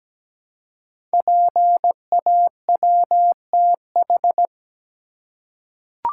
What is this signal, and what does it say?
Morse code, heard as a keyed tone, about 17 wpm, 700 Hz. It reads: PAWTH
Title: Code Groups - 5 Letters - 17wpm